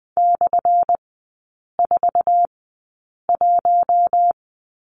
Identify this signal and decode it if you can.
Morse code, heard as a keyed tone, about 20 wpm, spaced as for 8 wpm, 700 Hz. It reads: /41
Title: 191 Lesson 31e - Letter v - Three Characters 20wpm